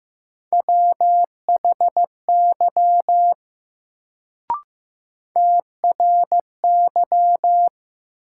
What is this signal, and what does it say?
Morse code, heard as a keyed tone, about 15 wpm, 700 Hz. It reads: WHY TRY
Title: Sentences from Top 200 Words - New Word Drill - 15wpm